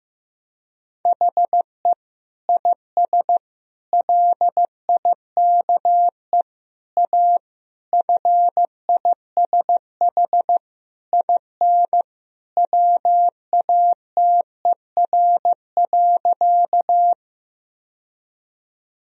Morse code, heard as a keyed tone, about 15 wpm, 700 Hz. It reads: HE IS LIKE A FISH IN WATER.
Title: Sentences from Top 1000 Words - Review Part 1 - 15wpm